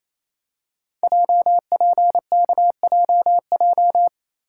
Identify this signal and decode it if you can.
Morse code, heard as a keyed tone, about 27 wpm, 700 Hz. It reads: JPKJJ
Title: Code Groups - 5 Letter:Numbers - 28wpm